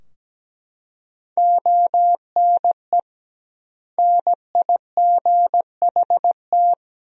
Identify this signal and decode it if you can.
Morse code, heard as a keyed tone, about 17 wpm, 700 Hz. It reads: ONE NIGHT
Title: 2x Word-Spacing - Sentences from Top 300 Words - 17wpm